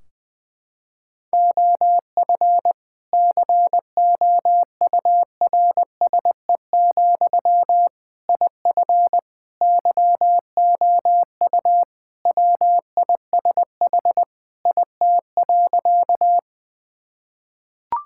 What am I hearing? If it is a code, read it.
Morse code, heard as a keyed tone, about 20 wpm, 700 Hz. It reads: OF COURSE, IF YOU WISH IT.
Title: Sentences from Top 1600 Words 20wpm